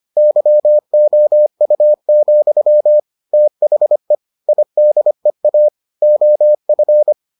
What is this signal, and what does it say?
Morse code, heard as a keyed tone, about 25 wpm, 600 Hz. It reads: YOU, THE IDEA OF